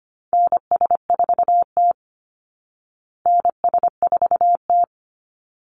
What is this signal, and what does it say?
Morse code, heard as a keyed tone, about 25 wpm, 700 Hz. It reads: NS4T NS4T
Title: Call Signs Repeated 3x - 25wpm